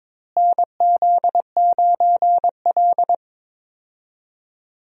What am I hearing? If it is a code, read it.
Morse code, heard as a keyed tone, about 21 wpm, 700 Hz. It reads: NZ9L
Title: USA Call Signs - 4 Characters - Repeated 3x - 22wpm